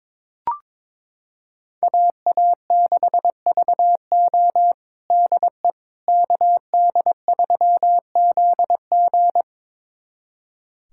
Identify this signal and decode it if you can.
Morse code, heard as a keyed tone, about 22 wpm, 700 Hz. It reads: AA6VO DE KD3ZG
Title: USA Callsign DE Callsign - 5 Characters - 22wpm